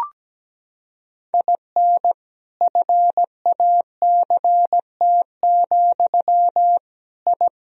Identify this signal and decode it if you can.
Morse code, heard as a keyed tone, about 17 wpm, 700 Hz. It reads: IN FACT, I
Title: Sentences from Top 1000 Words - Review Part 3 - 17wpm